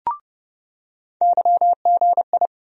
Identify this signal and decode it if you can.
Morse code, heard as a keyed tone, about 30 wpm, 700 Hz. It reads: YGI
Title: USA Call Sign Postfixes - 3 Letters 30wpm